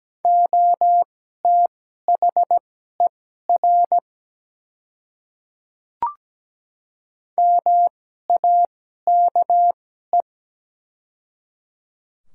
Morse code, heard as a keyed tone, about 17 wpm, spaced as for 12 wpm, 690 Hz. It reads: OTHER MAKE
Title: 2x Character Spacing - Top 100 Words - 17wpm